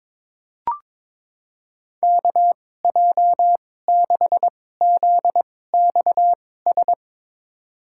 Morse code, heard as a keyed tone, about 21 wpm, spaced as for 16 wpm, 700 Hz. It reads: KJ6ZXS
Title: USA Call Signs - 2x Character Spacing - 22wpm